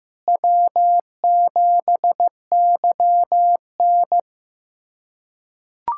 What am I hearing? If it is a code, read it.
Morse code, heard as a keyed tone, about 16 wpm, 700 Hz. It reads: W7YN
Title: USA Callsign DE Callsign - 4 Characters - 15wpm